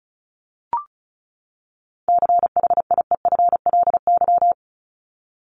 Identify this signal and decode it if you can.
Morse code, heard as a keyed tone, about 34 wpm, 700 Hz. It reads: CHIEFLY
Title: LY Suffix - Top 100 Words with Base Words 35wpm